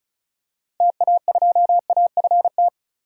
Morse code, heard as a keyed tone, about 36 wpm, 700 Hz. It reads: TA2AFT
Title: DXCC-100 Call Signs - 6 Characters - 35wpm